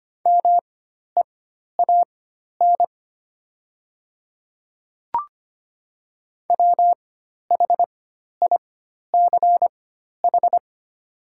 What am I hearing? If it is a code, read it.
Morse code, heard as a keyed tone, about 25 wpm, spaced as for 12 wpm, 700 Hz. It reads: MEAN WHICH
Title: 4x Character Spacing - Top 500 Words - 25wpm